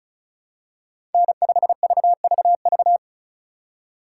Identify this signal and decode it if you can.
Morse code, heard as a keyed tone, about 36 wpm, 700 Hz. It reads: N5VVV
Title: USA Call Signs - 5 Characters - 4x Character Spacing - 35wpm